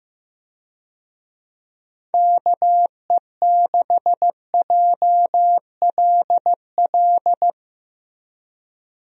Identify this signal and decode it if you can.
Morse code, heard as a keyed tone, about 15 wpm, 700 Hz. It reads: KE6JLL
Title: USA Call Signs - 6 Characters - Repeated 3x - 15wpm